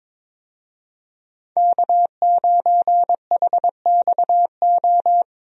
Morse code, heard as a keyed tone, about 22 wpm, 700 Hz. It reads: K9HXO
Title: USA Call Signs - 5 Characters - 22wpm